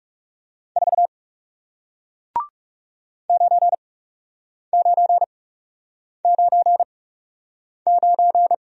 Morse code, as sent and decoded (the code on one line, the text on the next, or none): ....- / ----. ----. ----. ----.
4 9999